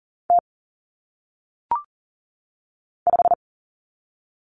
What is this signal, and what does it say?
Morse code, heard as a keyed tone, about 40 wpm, 700 Hz. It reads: T 5
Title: Common Cut Numbers - 1-Digit 40wpm